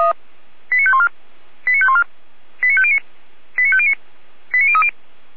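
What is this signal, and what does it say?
Five-tone selective call (ZVEI), heard as quick runs of five tones, tones of about 70 ms. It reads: 89514 89514 89509 89509 89030
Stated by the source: ZVEI five-tone selective call received on SDR, from the Feuerwehr Orlamünde fire brigade audio archive.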